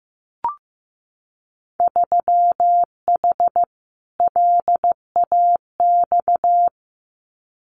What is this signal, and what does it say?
Morse code, heard as a keyed tone, about 15 wpm, 700 Hz. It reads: 3H LAX
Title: Winter Field Day Exchanges 15wpm